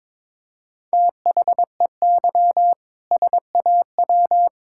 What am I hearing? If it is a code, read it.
Morse code, heard as a keyed tone, about 22 wpm, 700 Hz. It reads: THEY SAW
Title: Sentences from Top 1600 Words 22wpm